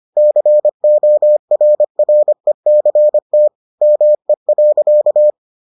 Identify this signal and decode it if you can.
Morse code, heard as a keyed tone, about 25 wpm, 600 Hz. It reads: CORRECT ME.